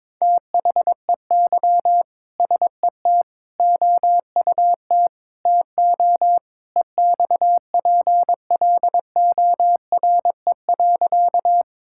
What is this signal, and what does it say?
Morse code, heard as a keyed tone, about 22 wpm, 700 Hz. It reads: THEY SET OUT TO EXPLORE.
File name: practice / 22wpm / Sentences with 2-Word Phrases - 22wpm